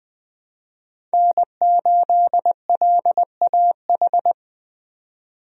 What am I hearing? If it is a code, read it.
Morse code, heard as a keyed tone, about 20 wpm, 700 Hz. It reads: N8LAH
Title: USA Call Signs - 5 Characters - 20wpm